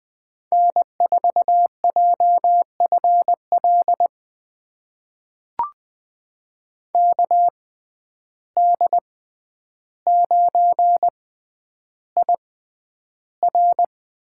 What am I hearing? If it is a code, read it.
Morse code, heard as a keyed tone, about 20 wpm, spaced as for 10 wpm, 700 Hz. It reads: N4JFL KD9IR
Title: USA Call Signs - 6x Character Spacing - 20wpm